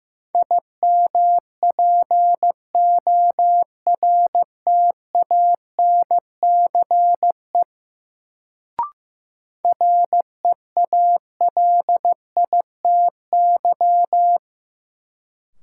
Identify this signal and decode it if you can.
Morse code, heard as a keyed tone, about 15 wpm, 700 Hz. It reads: IMPORTANCE REALITY
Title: Sentences from Top 1200 Words - New Word Drill - 15wpm